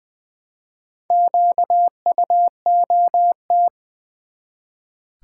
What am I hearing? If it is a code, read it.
Morse code, heard as a keyed tone, about 20 wpm, 700 Hz. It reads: QUOT
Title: Build A Word - 1 Letter At A Time - 3 to 5-Letter Words - 20wpm